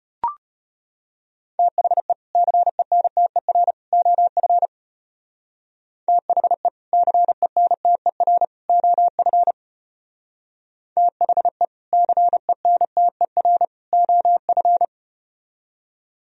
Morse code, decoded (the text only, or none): THE CENTER OF THE CENTER OF THE CENTER OF